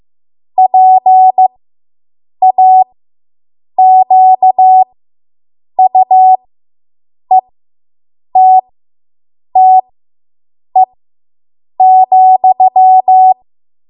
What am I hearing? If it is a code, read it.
Morse code, heard as a keyed tone, about 15 wpm, spaced as for 7 wpm, 750 Hz. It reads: PAQUETTE,